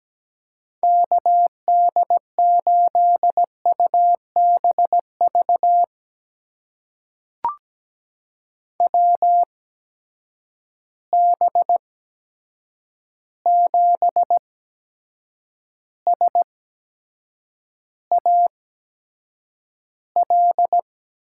Morse code, heard as a keyed tone, about 17 wpm, spaced as for 7 wpm, 700 Hz. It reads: KD8UBV WB7SAL
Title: USA Call Signs - 6 Characters - 8x Character Spacing - 17wpm